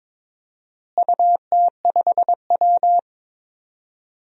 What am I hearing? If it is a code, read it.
Morse code, heard as a keyed tone, about 22 wpm, 700 Hz. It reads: UT5W
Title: DXCC-100 Call Signs - 4 Characters - Repeated 2x - 22wpm